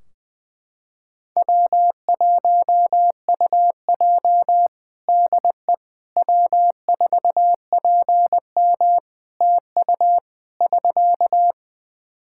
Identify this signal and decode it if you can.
Morse code, heard as a keyed tone, about 20 wpm, 700 Hz. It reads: W1UJ DE W4PM TU <SK>
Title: K1USN SST QSO Practice 20wpm